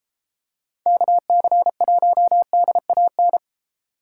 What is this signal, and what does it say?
Morse code, heard as a keyed tone, about 29 wpm, 700 Hz. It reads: KC1DAN